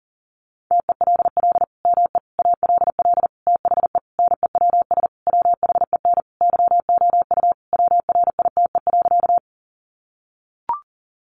Morse code, heard as a keyed tone, about 40 wpm, 700 Hz. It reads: TELL ME ALL THE NEWS WHEN YOU WRITE.
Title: Sentences from Top 800 Words 40wpm